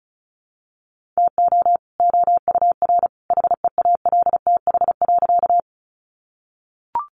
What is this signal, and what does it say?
Morse code, heard as a keyed tone, about 34 wpm, 700 Hz. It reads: TO OUR HEALTH.
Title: Sentences from Top 600 Words 35wpm